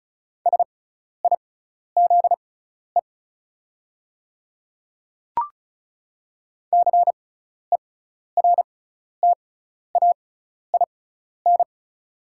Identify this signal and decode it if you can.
Morse code, heard as a keyed tone, about 36 wpm, spaced as for 12 wpm, 690 Hz. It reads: SIZE CERTAIN
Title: 6x Character Spacing - Top 500 Words - 35wpm